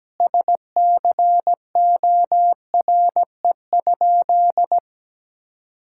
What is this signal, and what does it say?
Morse code, heard as a keyed tone, about 17 wpm, 700 Hz. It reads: SCORE?
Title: Sentences from Top 1600 Words 17wpm